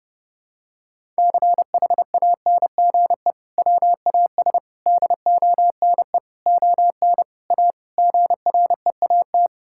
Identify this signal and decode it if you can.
Morse code, heard as a keyed tone, about 29 wpm, 700 Hz. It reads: CHANGE WAS DONE ON A GREAT
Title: Sentences from Top 1000 Words - Review Part 5 - 30wpm